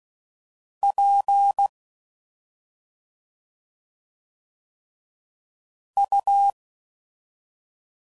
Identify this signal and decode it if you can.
Morse code, heard as a keyed tone, about 16 wpm, 800 Hz. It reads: P U